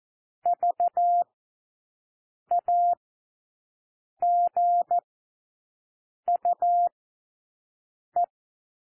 Morse code, heard as a keyed tone, about 14 wpm, spaced as for 6 wpm, 700 Hz. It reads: VAGUE
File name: cwmp3 / W1 SOUND WORDS 14 wpm 0